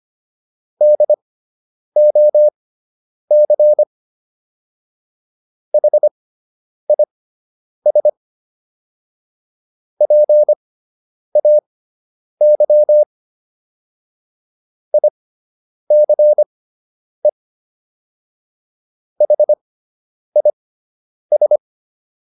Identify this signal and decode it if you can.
Morse code, heard as a keyed tone, about 25 wpm, spaced as for 9 wpm, 600 Hz. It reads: DOC HIS PAY ICE HIS